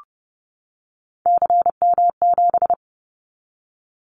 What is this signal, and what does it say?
Morse code, heard as a keyed tone, about 30 wpm, 700 Hz. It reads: CM7
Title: Callsign Prefixes for Top 100 DXCC - 30wpm